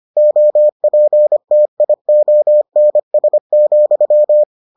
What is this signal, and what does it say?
Morse code, heard as a keyed tone, about 25 wpm, 600 Hz. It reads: OPTIONS,